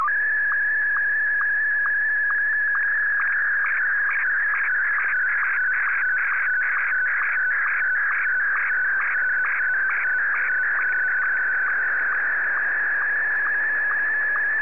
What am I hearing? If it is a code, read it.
SSTV (slow-scan television), Martin 1.
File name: sstv / M1-VK3